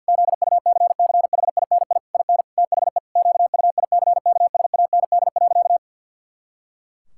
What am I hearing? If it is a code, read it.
Morse code, heard as a keyed tone, about 50 wpm, 700 Hz. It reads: ZUCCHINI IN THE QUICKSAND.
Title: Pangram Challenge v2 - 50wpm